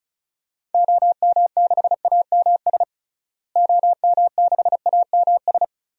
Morse code, heard as a keyed tone, about 36 wpm, 700 Hz. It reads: OM6AMS OM6AMS
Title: DXCC-100 Call Signs - 6 Characters - Repeated 2x - 35wpm